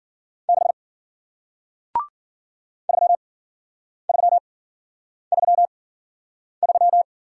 Morse code, heard as a keyed tone, about 53 wpm, spaced as for 9 wpm, 700 Hz. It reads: 6 3333